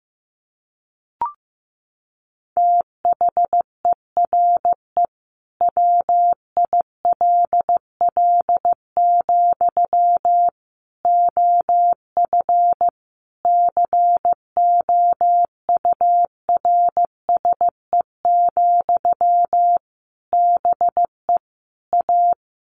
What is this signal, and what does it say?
Morse code, heard as a keyed tone, about 15 wpm, 700 Hz. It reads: THERE WILL, OF COURSE, BE A
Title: Sentences from Top 1400 Words 15wpm